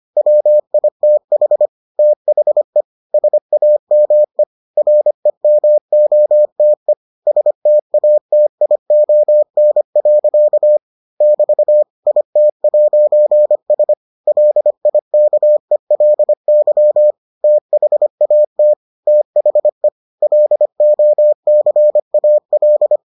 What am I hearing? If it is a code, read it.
Morse code, heard as a keyed tone, about 25 wpm, 600 Hz. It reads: WITH THE SAME REMOTE STATION. <BT> IT'S LIKELY THAT THE LOCAL